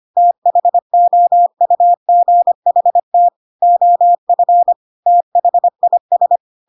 Morse code, heard as a keyed tone, about 25 wpm, 700 Hz. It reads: THOUGHT OF THIS
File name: Pinocchio / Pinocchio ch3 25 25 ik1hge